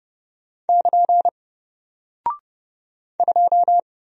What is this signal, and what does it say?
Morse code, heard as a keyed tone, about 30 wpm, 700 Hz. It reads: <KN> 2
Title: Single Character (Letter:Number:Punctuation:Prosign) Alt 30wpm